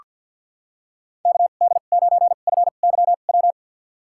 Morse code, heard as a keyed tone, about 47 wpm, spaced as for 35 wpm, 700 Hz. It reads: KD9FYW
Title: USA Call Signs - 2x Character Spacing - 50wpm